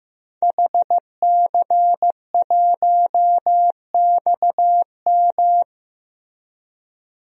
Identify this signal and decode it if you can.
Morse code, heard as a keyed tone, about 16 wpm, 700 Hz. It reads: HC1XM
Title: DXCC-100 Call Signs - 5 Characters - Repeated 3x - 15wpm